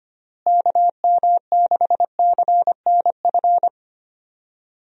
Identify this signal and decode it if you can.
Morse code, heard as a keyed tone, about 25 wpm, 700 Hz. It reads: KM6CNF
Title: USA Call Signs - 6 Characters - Repeated 2x - 25wpm